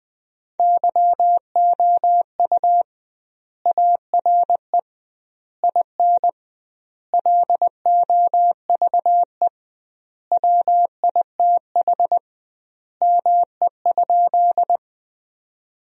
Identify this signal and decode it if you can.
Morse code, heard as a keyed tone, about 20 wpm, 700 Hz. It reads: YOU ARE IN LOVE WITH ME?
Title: 2x Word-Spacing - Sentences from Top 400 Words - 20wpm